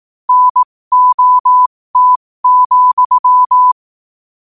Morse code, heard as a keyed tone, about 18 wpm, spaced as for 15 wpm, 1000 Hz. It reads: NOT,
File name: qotdfiles / 15wpm